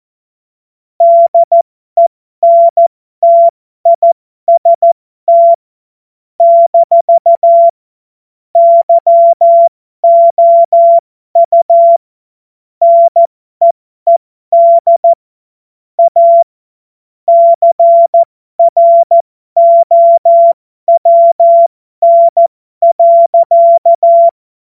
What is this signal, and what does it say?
Morse code, heard as a keyed tone, about 14 wpm, spaced as for 12 wpm, 680 Hz. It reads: DENTIST - YOU NEED A CROWN.
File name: Morsekurs / Witze